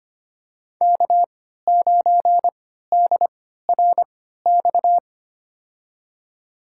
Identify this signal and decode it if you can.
Morse code, heard as a keyed tone, about 25 wpm, spaced as for 14 wpm, 700 Hz. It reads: K9DRX
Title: USA Call Signs - 3x Character Spacing - 25wpm